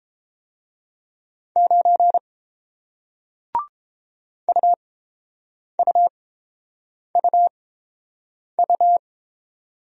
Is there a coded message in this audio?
Yes: Morse code from 1.6 to 9.0 s, heard as a keyed tone, about 31 wpm, spaced as for 7 wpm, 700 Hz. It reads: 9 UUUU